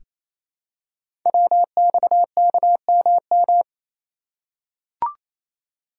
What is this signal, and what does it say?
Morse code, heard as a keyed tone, about 27 wpm, 700 Hz. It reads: WXKMM
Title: Code Groups - 5 Letters - 28wpm